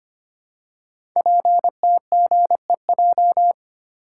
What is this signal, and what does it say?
Morse code, heard as a keyed tone, about 25 wpm, 700 Hz. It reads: PTGEJ
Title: Code Groups - 5 Letters - 25wpm